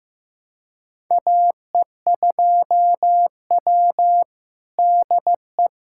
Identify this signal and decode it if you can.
Morse code, heard as a keyed tone, about 15 wpm, 700 Hz. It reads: AE2W DE
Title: USA Callsign DE Callsign - 4 Characters - 15wpm